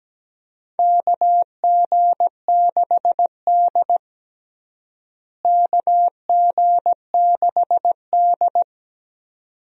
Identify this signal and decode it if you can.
Morse code, heard as a keyed tone, about 17 wpm, 700 Hz. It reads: KG6D KG6D
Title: USA Call Signs - 4 Characters - Repeated 2x - 17wpm